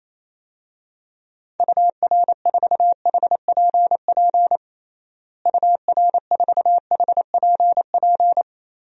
Morse code, heard as a keyed tone, about 28 wpm, 700 Hz. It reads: UR4HPP UR4HPP
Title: DXCC-100 Call Signs - 6 Characters - Repeated 3x - 28wpm